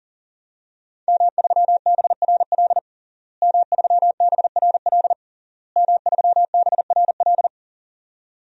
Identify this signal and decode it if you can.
Morse code, heard as a keyed tone, about 40 wpm, 700 Hz. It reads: M3BRL M3BRL M3BRL
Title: DXCC-100 Call Signs - 5 Characters - Repeated 3x - 40wpm